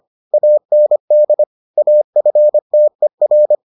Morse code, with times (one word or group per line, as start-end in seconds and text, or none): 0.3-1.4 AND
1.8-3.5 AFTER